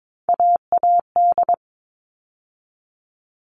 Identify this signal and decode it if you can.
Morse code, heard as a keyed tone, about 22 wpm, 700 Hz. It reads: AAD
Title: All Cut Numbers - 3-Digit 22wpm